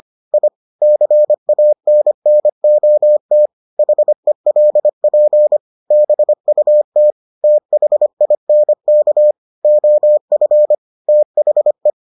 Morse code, heard as a keyed tone, about 25 wpm, 600 Hz. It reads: I CANNOT HELP BUT THINK OF THE